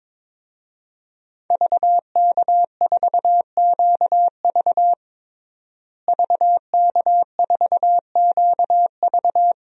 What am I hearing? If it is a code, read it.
Morse code, heard as a keyed tone, about 22 wpm, 700 Hz. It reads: VK4QV VK4QV
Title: DXCC-100 Call Signs - 5 Characters - Repeated 3x - 22wpm